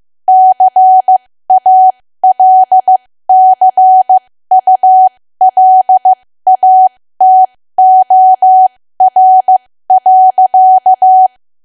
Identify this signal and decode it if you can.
Morse code, heard as a keyed tone, about 15 wpm, spaced as for 13 wpm, 750 Hz. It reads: CALCULATOR.
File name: cwa / W1AW-Level2 / 13WPM